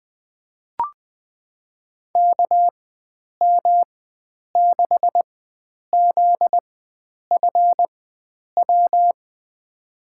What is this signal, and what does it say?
Morse code, heard as a keyed tone, about 21 wpm, spaced as for 9 wpm, 700 Hz. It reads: KM6ZFW
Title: USA Call Signs - 6 Characters - 4x Character Spacing - 20wpm